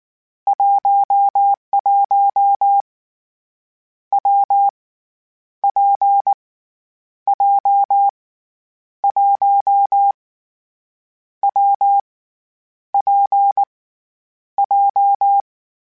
Morse code, heard as a keyed tone, about 19 wpm, 800 Hz. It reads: 11 W P J 1 W P J
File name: Variable Speed / Variable Speed-mp3s / WPJ WPJ1 sp @18